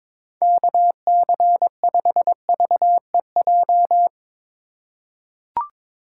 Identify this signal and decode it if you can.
Morse code, heard as a keyed tone, about 22 wpm, 700 Hz. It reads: KC5VEJ
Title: USA Call Signs - 6 Characters - 2x Character Spacing - 22wpm